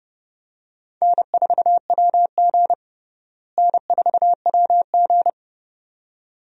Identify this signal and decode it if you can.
Morse code, heard as a keyed tone, about 32 wpm, 700 Hz. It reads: N4WG N4WG
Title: USA Call Signs - 4 Characters - Repeated 2x - 30wpm